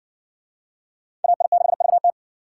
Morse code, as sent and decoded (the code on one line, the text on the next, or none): .- .. -.... ...- -
AI6VT